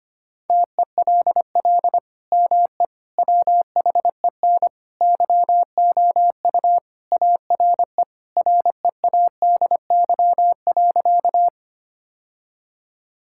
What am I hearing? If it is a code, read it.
Morse code, heard as a keyed tone, about 25 wpm, 700 Hz. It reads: TELL ME WHEN YOU ARE READY.
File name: practice / 25wpm / Sentences with 2-Word Phrases - 25wpm